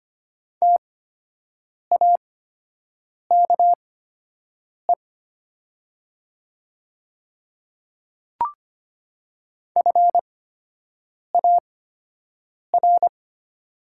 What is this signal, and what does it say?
Morse code, heard as a keyed tone, about 25 wpm, spaced as for 7 wpm, 700 Hz. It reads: TAKE FAR
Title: 8x Character Spacing - Top 300 Words - 25wpm